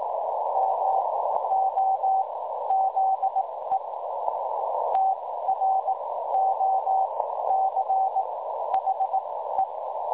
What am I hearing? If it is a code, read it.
Morse code, heard as a keyed tone, about 18 wpm, spaced as for 13 wpm, 780 Hz, with noise. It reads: MOZE TROCHE
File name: dlazaawansowanych / nagraniazpasma / websdr recording 3565.0kHz